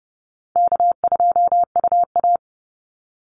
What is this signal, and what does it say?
Morse code, heard as a keyed tone, about 30 wpm, 700 Hz. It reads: K2UA
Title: Easy Call Signs Repeated 2x - 30wpm